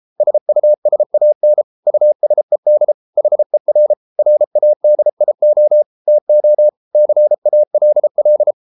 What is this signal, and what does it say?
Morse code, heard as a keyed tone, about 33 wpm, 600 Hz. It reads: SUSAN USED HER RADIO TO CALL